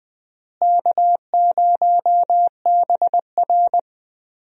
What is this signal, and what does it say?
Morse code, heard as a keyed tone, about 21 wpm, 700 Hz. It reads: K0BR
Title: USA Call Signs - 4 Characters - Repeated 2x - 20wpm